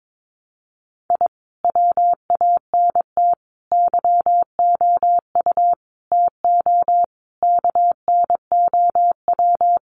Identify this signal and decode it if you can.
Morse code, heard as a keyed tone, about 22 wpm, 700 Hz. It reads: I WANT YOU TO KNOW
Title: Sets of 5 Words Encore 22wpm